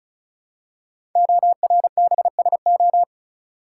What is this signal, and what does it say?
Morse code, heard as a keyed tone, about 36 wpm, 700 Hz. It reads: ORBSO
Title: Code Groups - 5 Letter:Numbers - 35wpm